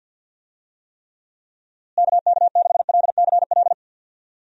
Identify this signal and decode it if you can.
Morse code, heard as a keyed tone, about 49 wpm, 700 Hz. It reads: KK6LCL
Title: USA Call Signs - 6 Characters - Repeated 2x - 50wpm